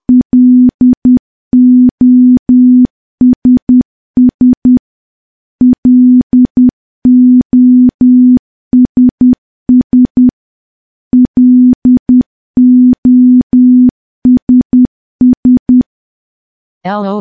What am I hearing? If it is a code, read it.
Morse code, heard as a keyed tone, about 10 wpm, 250 Hz. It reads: LOSS LOSS LOSS
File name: morse / morse-training-10WPM